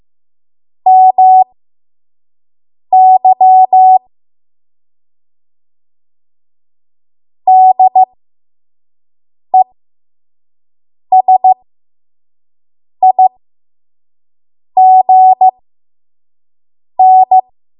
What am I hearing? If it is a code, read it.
Morse code, heard as a keyed tone, about 15 wpm, spaced as for 5 wpm, 750 Hz. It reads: MY DESIGN
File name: Morse / 05WPM